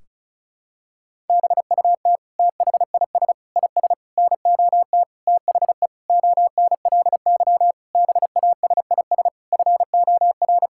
Morse code, heard as a keyed tone, about 36 wpm, 700 Hz. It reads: BUT THIS IS NOT THE ONLY BASIS FOR
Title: Sentences from Top 1600 Words 35wpm